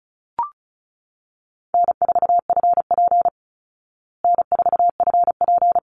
Morse code, heard as a keyed tone, about 34 wpm, 700 Hz. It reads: N4FP N4FP
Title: Easy Call Signs Repeated 2x - 35wpm